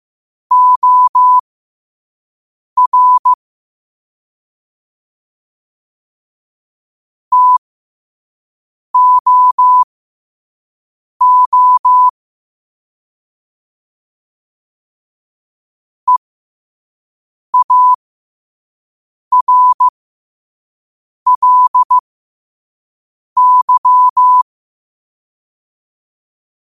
Morse code, heard as a keyed tone, about 15 wpm, spaced as for 5 wpm, 1000 Hz. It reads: OR TOO EARLY
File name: qotdfiles / 5wpm